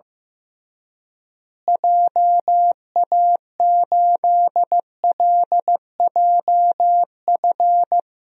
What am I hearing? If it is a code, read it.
Morse code, heard as a keyed tone, about 15 wpm, 700 Hz. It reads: JA8LJF
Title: DXCC-100 Call Signs - 6 Characters - Repeated 3x - 15wpm